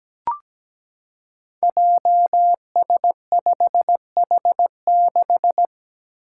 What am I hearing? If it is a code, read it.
Morse code, heard as a keyed tone, about 17 wpm, 700 Hz. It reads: JS5H6
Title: Five Challenging Letter-Number Combinations 17wpm